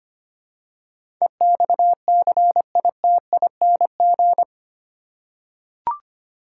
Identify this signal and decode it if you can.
Morse code, heard as a keyed tone, about 25 wpm, 700 Hz. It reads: EXCITING
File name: Eight Letter Words - 25wpm